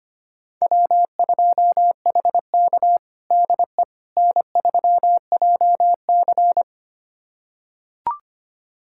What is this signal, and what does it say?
Morse code, heard as a keyed tone, about 25 wpm, 700 Hz. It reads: W2HK DE N3JC
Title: USA Callsign DE Callsign - 4 Characters - 25wpm